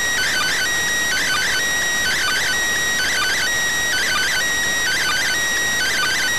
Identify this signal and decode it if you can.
Two-tone teleprinter signal, about 400 Hz shift.